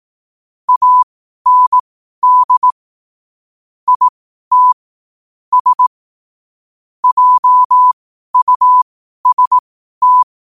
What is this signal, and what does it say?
Morse code, heard as a keyed tone, about 18 wpm, spaced as for 12 wpm, 1000 Hz. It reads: AND IT S JUST
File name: qotdfiles / 12wpm